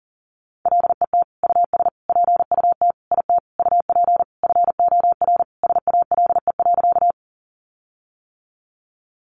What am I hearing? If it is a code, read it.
Morse code, heard as a keyed tone, about 40 wpm, 700 Hz. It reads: LET US PUT IT UP FOR SALE.
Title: Sentences from Top 1400 Words 40wpm